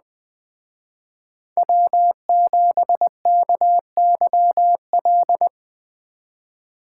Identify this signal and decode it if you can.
Morse code, heard as a keyed tone, about 21 wpm, 700 Hz. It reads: W7KYL
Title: Speed Racing - USA 1x3 Call Signs 17wpm